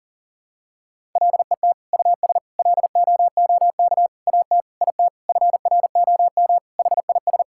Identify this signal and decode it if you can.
Morse code, heard as a keyed tone, about 40 wpm, 700 Hz. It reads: LET US LOOK AT IT FROM HIS
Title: Sentences from Top 1000 Words - Review Part 6 - 40wpm